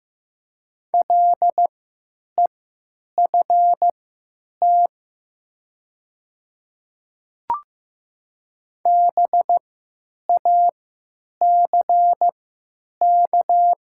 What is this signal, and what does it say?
Morse code, heard as a keyed tone, about 15 wpm, spaced as for 9 wpm, 700 Hz. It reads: LEFT BACK
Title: 3x Character Spacing - Top 500 Words - 15wpm